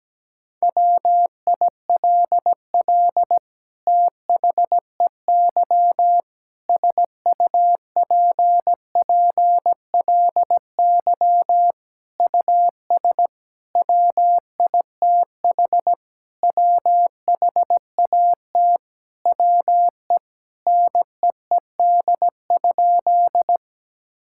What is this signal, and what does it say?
Morse code, heard as a keyed tone, about 17 wpm, 700 Hz. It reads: WILL THEY SUPPLY US WITH WHAT WE NEED?
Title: Sentences from Top 1000 Words - Review Part 5 - 17wpm